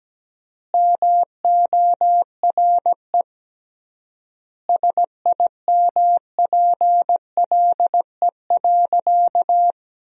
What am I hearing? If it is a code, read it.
Morse code, heard as a keyed tone, about 17 wpm, 700 Hz. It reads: MORE SIMPLE.
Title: 3x Word-Spacing - Sentences from Top 500 Words - 17wpm